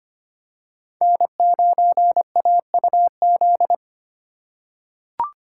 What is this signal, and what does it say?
Morse code, heard as a keyed tone, about 25 wpm, 700 Hz. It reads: N9AUZ
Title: USA Call Signs - 5 Characters - 2x Character Spacing - 25wpm